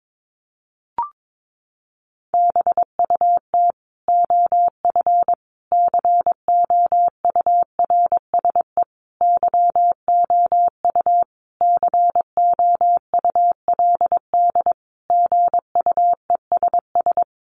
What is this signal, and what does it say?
Morse code, heard as a keyed tone, about 22 wpm, 700 Hz. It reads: BUT OF COURSE YOU COULD GUESS
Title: Sentences from Top 1500 Words 22wpm